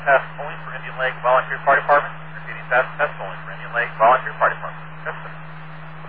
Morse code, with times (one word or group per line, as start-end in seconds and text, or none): none